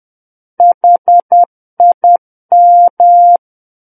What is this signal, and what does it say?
Morse code, heard as a keyed tone, about 10 wpm, 700 Hz. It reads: HIM
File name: Book One / 01-Chapter 1 - 10 WPM